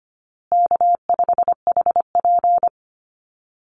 Morse code, heard as a keyed tone, about 25 wpm, 700 Hz. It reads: K5HP
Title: Call Signs Repeated 3x - 25wpm